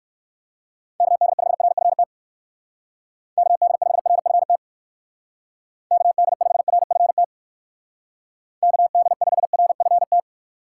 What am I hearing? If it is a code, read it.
Morse code, heard as a keyed tone, about 59 wpm, 700 Hz. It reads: KD5RFT KD5RFT KD5RFT KD5RFT